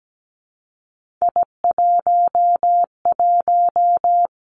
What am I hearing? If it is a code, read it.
Morse code, heard as a keyed tone, about 17 wpm, 700 Hz. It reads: I11
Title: Three Challenging Letter-Number Combinations 17wpm